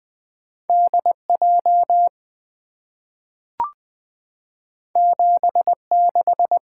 Morse code, heard as a keyed tone, about 20 wpm, 690 Hz. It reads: DJ 76
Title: Two Letter-Number Combinations - 20wpm